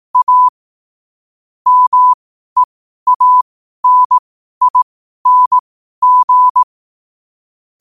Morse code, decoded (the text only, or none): A MEANING